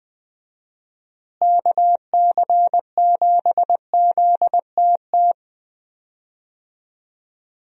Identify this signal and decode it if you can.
Morse code, heard as a keyed tone, about 19 wpm, 700 Hz. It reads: KC7ZTT
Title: USA Call Signs - 6 Characters - Repeated 3x - 20wpm